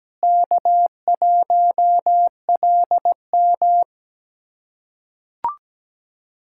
Morse code, heard as a keyed tone, about 17 wpm, 700 Hz. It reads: K1LM
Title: USA Callsign DE Callsign - 4 Characters - 17wpm